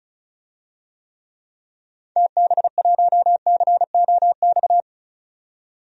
Morse code, heard as a keyed tone, about 36 wpm, 700 Hz. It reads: TB1COX
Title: DXCC-100 Call Signs - 6 Characters - 35wpm